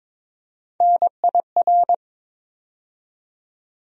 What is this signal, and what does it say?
Morse code, heard as a keyed tone, about 22 wpm, 700 Hz. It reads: NIR
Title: USA Call Sign Postfixes - 3 Letters 22wpm